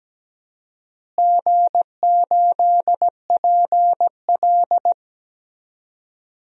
Morse code, heard as a keyed tone, about 17 wpm, 700 Hz. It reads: G8PL
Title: DXCC-100 Call Signs - 4 Characters - Repeated 3x - 17wpm